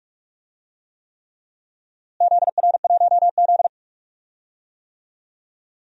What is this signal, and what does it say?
Morse code, heard as a keyed tone, about 42 wpm, 700 Hz. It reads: ZR1Z
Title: DXCC-100 Call Signs - 4 Characters - Repeated 3x - 45wpm